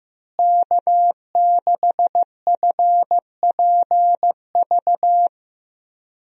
Morse code, heard as a keyed tone, about 15 wpm, 700 Hz. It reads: K6FPV
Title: USA Call Signs - 5 Characters - Repeated 3x - 15wpm